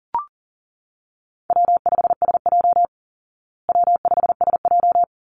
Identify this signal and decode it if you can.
Morse code, heard as a keyed tone, about 40 wpm, 700 Hz. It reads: W5SJ W5SJ
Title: Call Signs Repeated 3x - 40wpm